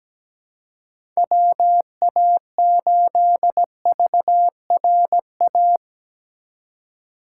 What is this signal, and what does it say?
Morse code, heard as a keyed tone, about 17 wpm, 700 Hz. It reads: WA8VRA
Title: USA Call Signs - 6 Characters - 4x Character Spacing - 17wpm